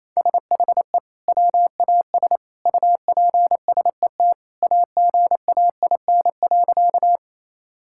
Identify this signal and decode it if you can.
Morse code, heard as a keyed tone, about 28 wpm, 700 Hz. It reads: SHE WAS UPSET AGAIN.
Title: Sentences from Top 1000 Words - Review Part 5 - 28wpm